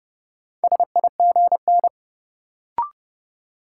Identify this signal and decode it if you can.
Morse code, heard as a keyed tone, about 30 wpm, 700 Hz. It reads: SIGN
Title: Sentences from Top 1000 Words - New Word Drill - 30wpm